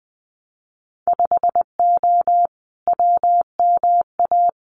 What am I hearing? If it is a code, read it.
Morse code, heard as a keyed tone, about 20 wpm, 700 Hz. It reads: 5O WMA
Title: Winter Field Day Exchanges Encore 20wpm